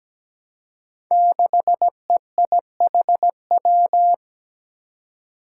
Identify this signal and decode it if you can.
Morse code, heard as a keyed tone, about 17 wpm, 700 Hz. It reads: 6EIHW
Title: Five Challenging Letter-Number Combinations 17wpm